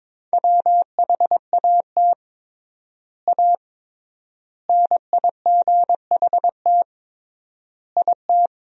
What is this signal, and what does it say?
Morse code, heard as a keyed tone, about 21 wpm, 700 Hz. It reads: WHAT A NIGHT IT
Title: 3x Word-Spacing - Sentences from Top 300 Words - 22wpm